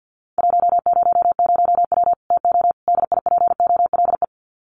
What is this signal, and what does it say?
Morse code, heard as a keyed tone, about 50 wpm, 700 Hz. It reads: 100W TO DIPOLE
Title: Common QSO Phrases 50wpm